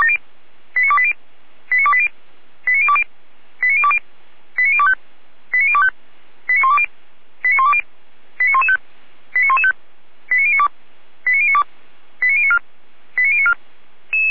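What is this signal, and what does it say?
Five-tone selective call (ZVEI), heard as quick runs of five tones, tones of about 70 ms. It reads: 89280 89280 89020 89020 89025 89025 89120 89120 89115 89115 89092 89092 89094 89094
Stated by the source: Five-tone ZVEI selective call received on an SDR by Feuerwehr Orlamünde (fire brigade).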